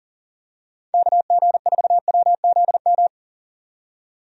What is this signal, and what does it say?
Morse code, heard as a keyed tone, about 40 wpm, 700 Hz. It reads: KG4WZM
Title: USA Call Signs - 6 Characters - 40wpm